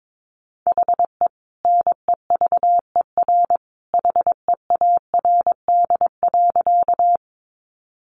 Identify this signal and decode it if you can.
Morse code, heard as a keyed tone, about 22 wpm, 700 Hz. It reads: HE NEVER HEARD.
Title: Sentences from Top 400 Words 22wpm